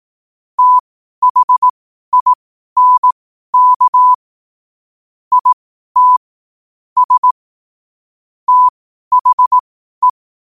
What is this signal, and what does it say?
Morse code, heard as a keyed tone, about 18 wpm, spaced as for 12 wpm, 1000 Hz. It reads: THINK IT S THE
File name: qotdfiles / 12wpm